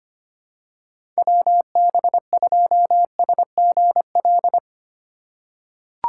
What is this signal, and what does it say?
Morse code, heard as a keyed tone, about 25 wpm, 700 Hz. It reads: WB2SGL
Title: USA Call Signs - 6 Characters - 3x Character Spacing - 25wpm